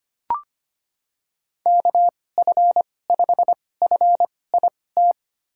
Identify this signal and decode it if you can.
Morse code, heard as a keyed tone, about 25 wpm, spaced as for 18 wpm, 700 Hz. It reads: KF5FIT
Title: USA Call Signs - 6 Characters - 2x Character Spacing - 25wpm